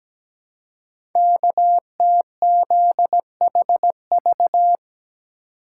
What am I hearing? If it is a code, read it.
Morse code, heard as a keyed tone, about 17 wpm, 700 Hz. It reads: KTZHV
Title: Code Groups - 5 Letters - 17wpm